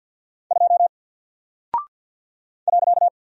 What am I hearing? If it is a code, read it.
Morse code, heard as a keyed tone, about 50 wpm, 700 Hz. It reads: 2 .
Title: Single Character (Letter:Number:Punctuation:Prosign) 50wpm